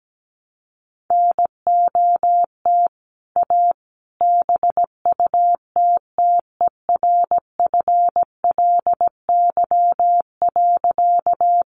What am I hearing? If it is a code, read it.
Morse code, heard as a keyed tone, about 17 wpm, 700 Hz. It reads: NOT A BUTTERFLY.
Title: A Sound Of Thunder 17wpm